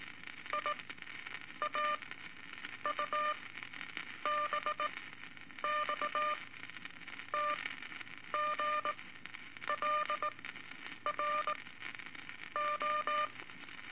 Morse code, heard as a keyed tone, about 19 wpm, spaced as for 8 wpm, 1200 Hz, with radio noise. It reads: IAUBX TGLR O
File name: Morse Code Records / 17A LCM 162 Eight Words Per Minute